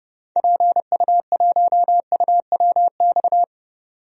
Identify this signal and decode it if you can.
Morse code, heard as a keyed tone, about 32 wpm, 700 Hz. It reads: PU1UWX
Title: DXCC-100 Call Signs - 6 Characters - 30wpm